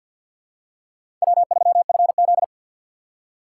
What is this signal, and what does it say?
Morse code, heard as a keyed tone, about 53 wpm, 700 Hz. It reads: W3FZ
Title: USA Call Signs - 4 Characters - 2x Character Spacing - 50wpm